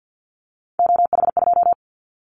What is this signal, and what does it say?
Morse code, heard as a keyed tone, about 50 wpm, 700 Hz. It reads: OH2
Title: Callsign Prefixes for Top 50 DXCC - 50wpm